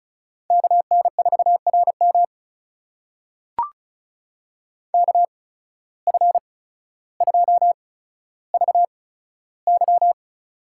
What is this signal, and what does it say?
Morse code, heard as a keyed tone, about 36 wpm, spaced as for 15 wpm, 700 Hz. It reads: KN4RM KF2VY
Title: USA Call Signs - 8x Character Spacing - 35wpm